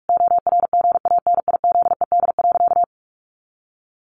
Morse code, heard as a keyed tone, about 44 wpm, 700 Hz. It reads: ORGANIZED.
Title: Sentences from Top 1300 Words 45wpm